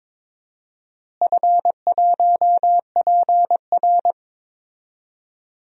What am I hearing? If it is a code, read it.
Morse code, heard as a keyed tone, about 22 wpm, 700 Hz. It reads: F1PR
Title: DXCC-100 Call Signs - 4 Characters - Repeated 3x - 22wpm